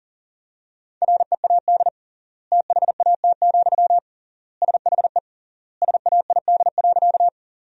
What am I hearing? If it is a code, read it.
Morse code, heard as a keyed tone, about 40 wpm, 700 Hz. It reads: READ THAT, SHE SAID.
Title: 3x Word-Spacing - Sentences from Top 200 Words - 40wpm